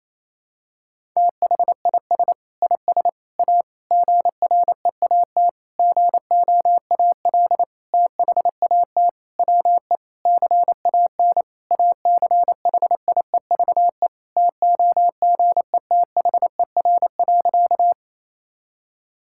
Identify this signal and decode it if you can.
Morse code, heard as a keyed tone, about 28 wpm, 700 Hz. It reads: THIS IS A GREAT GOAL THAT WE CAN ACHIEVE TOGETHER.
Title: Sentences from Top 1000 Words - Review Part 6 - 28wpm